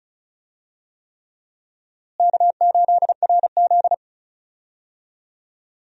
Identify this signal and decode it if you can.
Morse code, heard as a keyed tone, about 34 wpm, 700 Hz. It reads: K8RZ
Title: USA Call Signs - 4 Characters - Repeated 3x - 35wpm